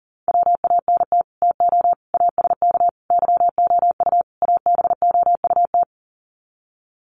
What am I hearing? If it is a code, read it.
Morse code, heard as a keyed tone, about 40 wpm, 700 Hz. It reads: WANT TO ASK YOU ABOUT
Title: Sets of 5 Words Encore2 40wpm